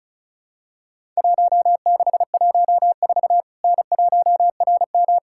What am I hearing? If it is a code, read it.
Morse code, heard as a keyed tone, about 35 wpm, 700 Hz. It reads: 1614 N1RM
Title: Worked All Europe DX Contest - QTC Exchange - 35wpm